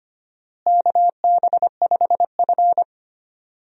Morse code, heard as a keyed tone, about 25 wpm, 700 Hz. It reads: KB5F
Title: USA Call Signs - 4 Characters - Repeated 3x - 25wpm